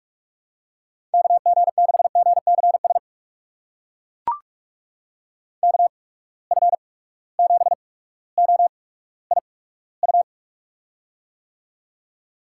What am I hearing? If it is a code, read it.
Morse code, heard as a keyed tone, about 42 wpm, spaced as for 19 wpm, 700 Hz. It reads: KG6GCS KF7OIU
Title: USA Call Signs - 6 Characters - 8x Character Spacing - 45wpm